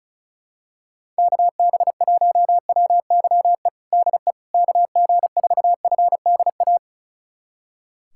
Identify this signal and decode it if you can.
Morse code, heard as a keyed tone, about 34 wpm, 690 Hz. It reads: KB1WYE DE KG4FDA
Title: USA Callsign DE Callsign - 6 Characters - 35wpm